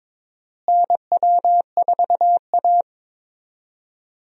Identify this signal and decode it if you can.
Morse code, heard as a keyed tone, about 22 wpm, 700 Hz. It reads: NW4A
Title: USA Call Signs - 4 Characters - Repeated 3x - 22wpm